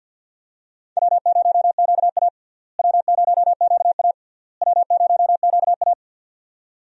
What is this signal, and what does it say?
Morse code, heard as a keyed tone, about 51 wpm, 700 Hz. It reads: W0QA W0QA W0QA
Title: USA Call Signs - 4 Characters - Repeated 3x - 50wpm